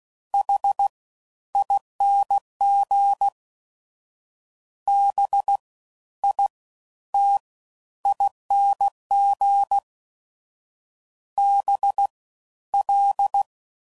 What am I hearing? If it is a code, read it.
Morse code, heard as a keyed tone, about 16 wpm, 800 Hz. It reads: H ING B I T ING B L
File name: Suffixes / Special INGs / Fast Ings